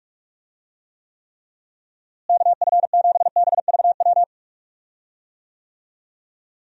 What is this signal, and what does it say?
Morse code, heard as a keyed tone, about 41 wpm, 700 Hz. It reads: KF7BVW
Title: USA Call Signs - 6 Characters - Repeated 3x - 45wpm